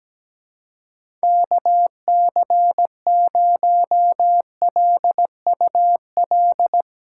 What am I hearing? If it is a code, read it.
Morse code, heard as a keyed tone, about 17 wpm, 700 Hz. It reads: KC0LUL
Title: USA Call Signs - 6 Characters - 2x Character Spacing - 17wpm